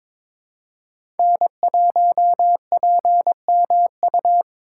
Morse code, heard as a keyed tone, about 22 wpm, 700 Hz. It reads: N1PMU
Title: USA Call Signs - 5 Characters - 3x Character Spacing - 22wpm